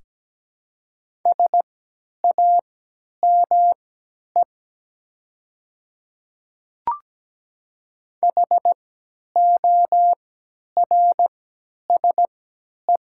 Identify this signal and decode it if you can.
Morse code, heard as a keyed tone, about 17 wpm, spaced as for 10 wpm, 700 Hz. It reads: SAME HORSE
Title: 3x Character Spacing - Top 300 Words - 17wpm